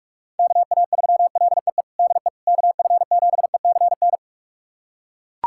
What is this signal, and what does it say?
Morse code, heard as a keyed tone, about 41 wpm, 690 Hz. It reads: KA3LEE DE KF7ECN
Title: USA Callsign DE Callsign - 6 Characters - 45wpm